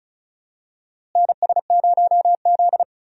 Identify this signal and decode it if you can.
Morse code, heard as a keyed tone, about 36 wpm, 700 Hz. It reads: NS0Z
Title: USA Call Signs - 4 Characters - Repeated 3x - 35wpm